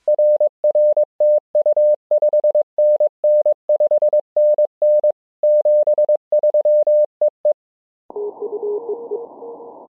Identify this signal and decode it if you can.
Morse code, heard as a keyed tone, about 22 wpm, 600 Hz. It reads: R R TU 5NN 5NN 73 E E
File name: ja5ffj-g8bcg